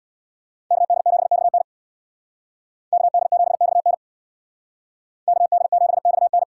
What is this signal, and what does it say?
Morse code, heard as a keyed tone, about 62 wpm, 700 Hz. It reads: KD7XN KD7XN KD7XN
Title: Speed Racing - USA 2x2 Call Signs 50wpm